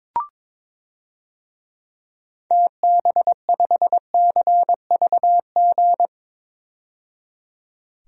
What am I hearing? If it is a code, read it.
Morse code, heard as a keyed tone, about 22 wpm, 700 Hz. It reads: TB5CVG
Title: DXCC-100 Call Signs - 6 Characters - 22wpm